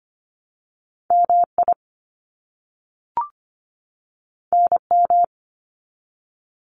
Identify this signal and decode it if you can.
Morse code, heard as a keyed tone, about 25 wpm, 700 Hz. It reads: MI NM
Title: US States 25wpm